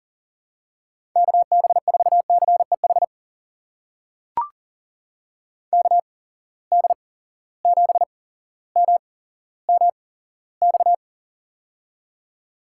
Morse code, heard as a keyed tone, about 37 wpm, spaced as for 17 wpm, 700 Hz. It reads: KB4CEH KD7MMX
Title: USA Call Signs - 6 Characters - 8x Character Spacing - 40wpm